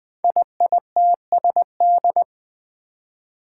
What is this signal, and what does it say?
Morse code, heard as a keyed tone, about 20 wpm, 700 Hz. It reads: IITSD
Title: Code Groups - 5 Letters - 20wpm